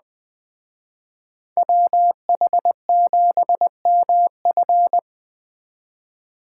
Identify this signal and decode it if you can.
Morse code, heard as a keyed tone, about 20 wpm, 700 Hz. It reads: WH7MF
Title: Speed Racing - USA 2x2 Call Signs 17wpm